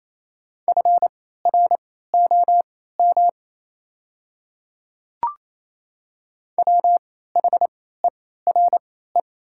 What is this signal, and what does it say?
Morse code, heard as a keyed tone, about 28 wpm, spaced as for 16 wpm, 700 Hz. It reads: FROM WHERE
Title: 3x Character Spacing - Top 500 Words - 28wpm